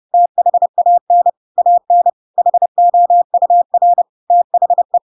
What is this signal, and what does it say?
Morse code, heard as a keyed tone, about 30 wpm, 700 Hz. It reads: THAN AN HOUR THE
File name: Pinocchio / Pinocchio ch8 30 30 ik1hge